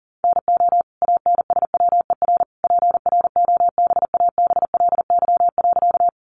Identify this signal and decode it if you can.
Morse code, heard as a keyed tone, about 40 wpm, 700 Hz. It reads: NO ANSWER PROBABLY.
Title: Sentences from Top 700 Words 40wpm